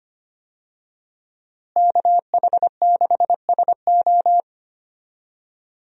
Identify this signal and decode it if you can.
Morse code, heard as a keyed tone, about 25 wpm, 700 Hz. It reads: KH6SO
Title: USA Call Signs - 5 Characters - Repeated 3x - 25wpm